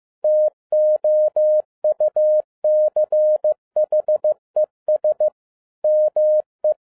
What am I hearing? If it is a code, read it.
Morse code, heard as a keyed tone, about 15 wpm, 610 Hz. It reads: TOUCHES ME
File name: FIELD DAY CONTEST QRV FROM SAKAE VILLAGE 15wpm